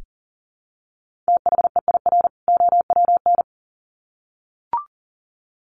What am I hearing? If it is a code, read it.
Morse code, heard as a keyed tone, about 40 wpm, 700 Hz. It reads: THEIR OWN
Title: Sets of 2 Words 40wpm